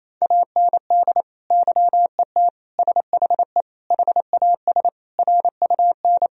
Morse code, heard as a keyed tone, about 28 wpm, 700 Hz. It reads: AND YET SHE HAS RUN
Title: Sentences from Top 1000 Words - Review Part 1 - 28wpm